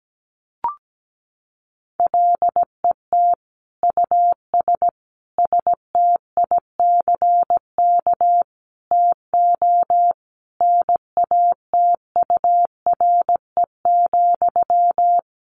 Morse code, heard as a keyed tone, about 17 wpm, 700 Hz. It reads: LET US STICK TO NATURE,
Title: Sentences from Top 1200 Words 17wpm